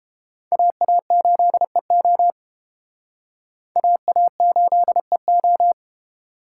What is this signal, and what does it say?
Morse code, heard as a keyed tone, about 32 wpm, 700 Hz. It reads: AA8EO AA8EO